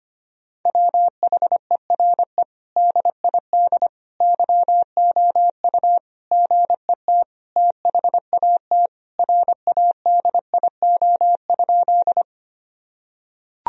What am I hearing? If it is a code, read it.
Morse code, heard as a keyed tone, about 25 wpm, 700 Hz. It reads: WHERE DID YOU GET THAT RADIO?
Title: Sentences from Top 1000 Words - Review Part 3 - 25wpm